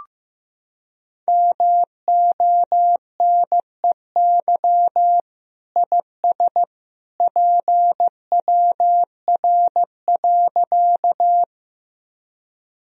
Morse code, heard as a keyed tone, about 16 wpm, 700 Hz. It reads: MONEY IS PWR.
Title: Sentences from Top 500 Words and Abbreviations 15wpm